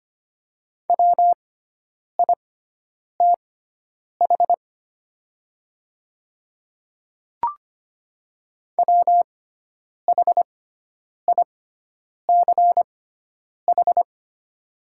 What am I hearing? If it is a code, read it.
Morse code, heard as a keyed tone, about 25 wpm, spaced as for 9 wpm, 700 Hz. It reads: WITH WHICH
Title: 6x Character Spacing - Top 100 Words - 25wpm